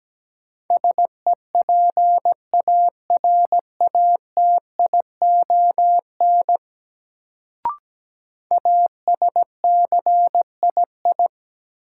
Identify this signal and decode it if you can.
Morse code, heard as a keyed tone, about 17 wpm, 700 Hz. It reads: SEPARATION ASCII
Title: Sets of 5 Words - New Word Drill - 17wpm